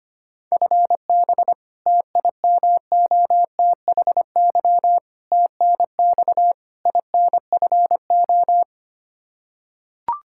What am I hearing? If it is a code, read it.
Morse code, heard as a keyed tone, about 25 wpm, 700 Hz. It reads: FB TIMOTHY TNX INFO
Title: K1USN SST QSO Practice 25wpm